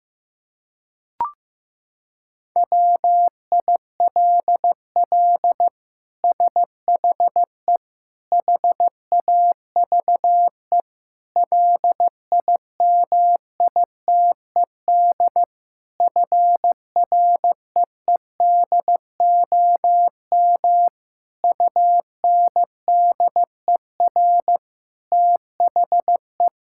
Morse code, heard as a keyed tone, about 15 wpm, 700 Hz. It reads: WILL SHE HAVE LIMITED FREEDOM UNDER THE
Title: Sentences from Top 1000 Words - Review Part 6 - 15wpm